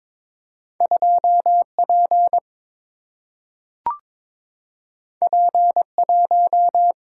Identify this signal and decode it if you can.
Morse code, heard as a keyed tone, about 22 wpm, 700 Hz. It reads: 2 P P 1
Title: Single-Letter Single-Number 22wpm